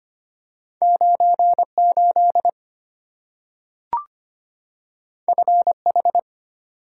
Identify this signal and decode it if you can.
Morse code, heard as a keyed tone, about 25 wpm, 700 Hz. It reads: 98 FH
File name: Two Letter-Number Combinations - 25wpm